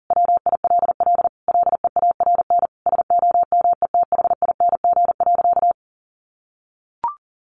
Morse code, heard as a keyed tone, about 40 wpm, 700 Hz. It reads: WILL LEARN SOMETHING.
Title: Sentences from Top 600 Words 40wpm